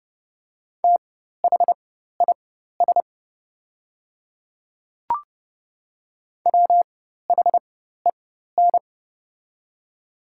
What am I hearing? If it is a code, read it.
Morse code, heard as a keyed tone, about 30 wpm, spaced as for 14 wpm, 700 Hz. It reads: THIS WHEN
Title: 4x Character Spacing - Top 200 Words - 30wpm